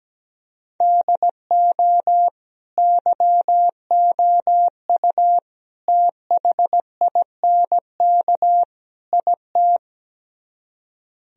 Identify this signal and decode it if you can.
Morse code, heard as a keyed tone, about 17 wpm, 700 Hz. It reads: DO YOU THINK IT
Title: Speed-Racing - Sets of 4 Words 17wpm